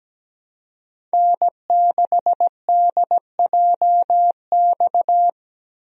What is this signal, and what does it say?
Morse code, heard as a keyed tone, about 17 wpm, 700 Hz. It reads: N6DJX
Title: USA Call Signs - 5 Characters - Repeated 2x - 17wpm